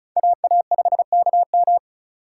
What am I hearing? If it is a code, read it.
Morse code, heard as a keyed tone, about 43 wpm, 700 Hz. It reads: AA5KM
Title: Speed Racing - USA 2x2 Call Signs 35wpm